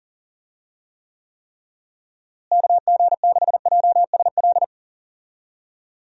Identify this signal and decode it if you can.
Morse code, heard as a keyed tone, about 40 wpm, 700 Hz. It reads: KG6JSL
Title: USA Call Signs - 6 Characters - Repeated 3x - 40wpm